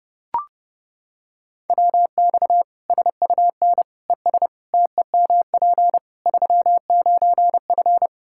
Morse code, heard as a keyed tone, about 32 wpm, 700 Hz. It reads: WX SUN ES TEMP 39F
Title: Practice QSO - Exchange 2 - Info - 30wpm